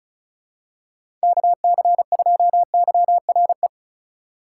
Morse code, heard as a keyed tone, about 36 wpm, 700 Hz. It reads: KC2YRE
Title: USA Call Signs - 6 Characters - 4x Character Spacing - 35wpm